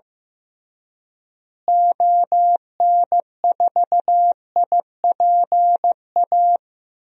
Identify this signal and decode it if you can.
Morse code, heard as a keyed tone, about 15 wpm, 700 Hz. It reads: ON4IPA
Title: DXCC-100 Call Signs - 6 Characters - Repeated 3x - 15wpm